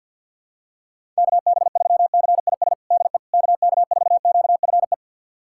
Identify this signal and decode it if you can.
Morse code, heard as a keyed tone, about 51 wpm, 700 Hz. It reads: KB3CIS DE KK4QFE
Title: USA Callsign DE Callsign - 6 Characters - 50wpm